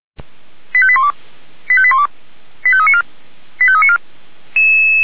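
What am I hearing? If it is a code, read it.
Five-tone selective call (ZVEI), heard as quick runs of five tones, tones of about 80 ms. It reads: 85812 85812 85394 85394
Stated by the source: Received on SDR; ZVEI five-tone selective call.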